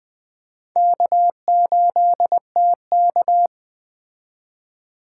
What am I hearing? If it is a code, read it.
Morse code, heard as a keyed tone, about 20 wpm, 700 Hz. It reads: K8TK
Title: Speed Racing - USA Call Signs 17wpm